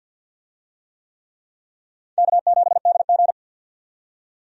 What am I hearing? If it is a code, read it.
Morse code, heard as a keyed tone, about 47 wpm, 700 Hz. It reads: K7DG
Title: USA Call Signs - 4 Characters - 50wpm